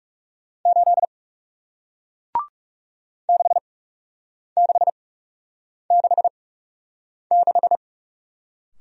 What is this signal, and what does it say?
Morse code, heard as a keyed tone, about 40 wpm, spaced as for 8 wpm, 700 Hz. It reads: 8 6666